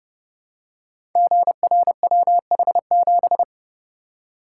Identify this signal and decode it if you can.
Morse code, heard as a keyed tone, about 29 wpm, 700 Hz. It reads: GRWH7
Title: Code Groups - 5 Letter:Numbers - 30wpm